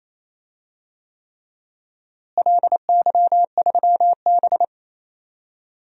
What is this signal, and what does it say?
Morse code, heard as a keyed tone, about 27 wpm, 700 Hz. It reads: LY3B
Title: DXCC-100 Call Signs - 4 Characters - Repeated 3x - 28wpm